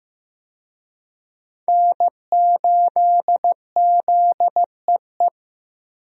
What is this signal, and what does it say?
Morse code, heard as a keyed tone, about 15 wpm, 700 Hz. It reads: N8ZEE
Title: USA Call Signs - 5 Characters - Repeated 3x - 15wpm